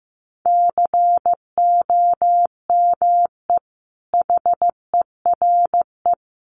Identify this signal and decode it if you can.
Morse code, heard as a keyed tone, about 15 wpm, 700 Hz. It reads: COME HERE
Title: Sentences from Top 800 Words 15wpm